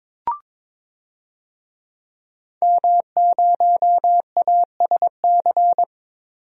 Morse code, heard as a keyed tone, about 22 wpm, 700 Hz. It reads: M0ASC
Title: DXCC-100 Call Signs - 5 Characters - Repeated 2x - 22wpm